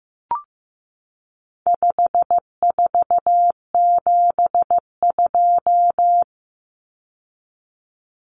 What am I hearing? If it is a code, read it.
Morse code, heard as a keyed tone, about 15 wpm, 700 Hz. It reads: 5472
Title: Common Cut Numbers - 4-Digit 15wpm